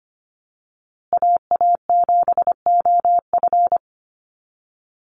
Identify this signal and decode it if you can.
Morse code, heard as a keyed tone, about 25 wpm, 700 Hz. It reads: AA7OF
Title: USA Call Signs Repeated 3x 25wpm